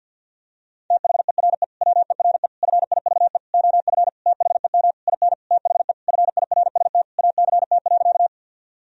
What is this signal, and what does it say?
Morse code, heard as a keyed tone, about 52 wpm, 700 Hz. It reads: THERE WERE FIVE OF THEM IN THE FIRST ACT.
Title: Sentences from Top 1000 Words - Review Part 2 - 50wpm